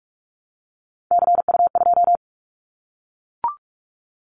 Morse code, heard as a keyed tone, about 44 wpm, 700 Hz. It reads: CU2
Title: Callsign Prefixes for Top 100 DXCC - 45wpm